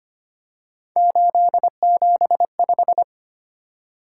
Morse code, heard as a keyed tone, about 25 wpm, 700 Hz. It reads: 875
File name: Three Letter-Number Combinations - 25wpm